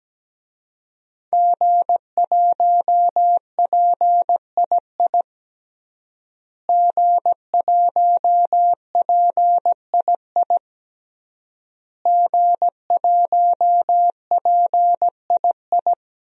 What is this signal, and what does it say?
Morse code, heard as a keyed tone, about 17 wpm, 700 Hz. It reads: G1PII G1PII G1PII
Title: DXCC-100 Call Signs - 5 Characters - Repeated 3x - 17wpm